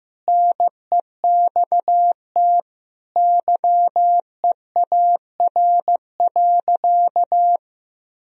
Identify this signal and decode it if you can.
Morse code, heard as a keyed tone, about 16 wpm, 700 Hz. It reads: NEXT YEAR.
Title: Sentences from Top 1000 Words - Review Part 3 - 15wpm